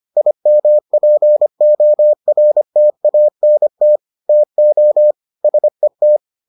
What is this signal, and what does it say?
Morse code, heard as a keyed tone, about 25 wpm, 600 Hz. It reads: IMPORTANT TO SET